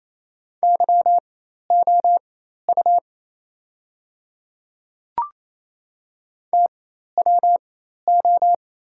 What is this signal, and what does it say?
Morse code, heard as a keyed tone, about 28 wpm, spaced as for 13 wpm, 700 Hz. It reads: YOU TWO
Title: 4x Character Spacing - Top 100 Words - 28wpm